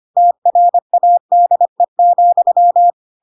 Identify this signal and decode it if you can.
Morse code, heard as a keyed tone, about 25 wpm, 700 Hz. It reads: TRADE,